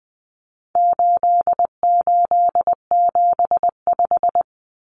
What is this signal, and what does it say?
Morse code, heard as a keyed tone, about 20 wpm, 700 Hz. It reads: 8875
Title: Common Cut Numbers - 4-Digit 20wpm